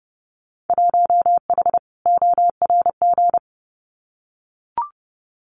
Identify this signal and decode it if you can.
Morse code, heard as a keyed tone, about 30 wpm, 700 Hz. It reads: 1H ORG
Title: Winter Field Day Exchanges 30wpm